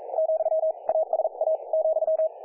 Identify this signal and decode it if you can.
Morse code, heard as a keyed tone, about 43 wpm, spaced as for 37 wpm, 640 Hz, with radio noise. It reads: , ASA ,
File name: dlazaawansowanych / nagraniazpasma / websdr recording 3564.7kHz